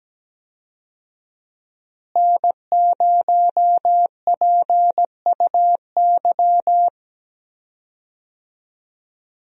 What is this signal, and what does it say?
Morse code, heard as a keyed tone, about 17 wpm, 700 Hz. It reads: N0PUY
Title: USA Call Signs - 5 Characters - Repeated 2x - 17wpm